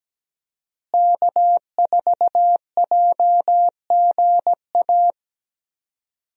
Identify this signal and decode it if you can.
Morse code, heard as a keyed tone, about 17 wpm, 700 Hz. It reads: K4JGA
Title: USA Call Signs - 5 Characters - Repeated 3x - 17wpm